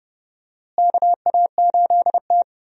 Morse code, heard as a keyed tone, about 29 wpm, 700 Hz. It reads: KA8T
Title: USA Call Signs - 4 Characters - Repeated 3x - 30wpm